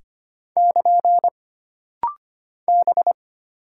Morse code, heard as a keyed tone, about 25 wpm, 700 Hz. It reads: <KN> B
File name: Single Character (Letter:Number:Punctuation:Prosign) - Rapid-Fire - 25wpm